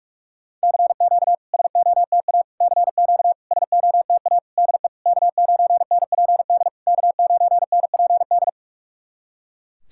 Morse code, heard as a keyed tone, about 44 wpm, 690 Hz. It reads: CQ SOTA CQ SOTA DE K9NPD K9NPD
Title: SOTA QSO Practice 45wpm